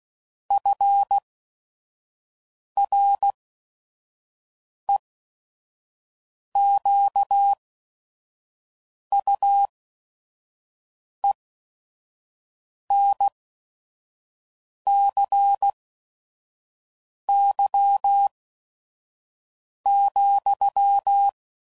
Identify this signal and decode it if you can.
Morse code, heard as a keyed tone, about 16 wpm, spaced as for 5 wpm, 800 Hz. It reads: FREQUENCY,
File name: CW Code Groups @5WPM / Technical Subjects / XL